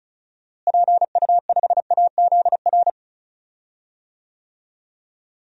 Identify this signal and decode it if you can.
Morse code, heard as a keyed tone, about 36 wpm, 700 Hz. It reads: PU5AZR
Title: DXCC-100 Call Signs - 6 Characters - Repeated 3x - 35wpm